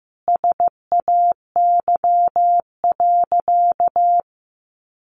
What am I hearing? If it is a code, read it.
Morse code, heard as a keyed tone, about 15 wpm, 700 Hz. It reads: SAY.
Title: Sentences from Top 1200 Words 15wpm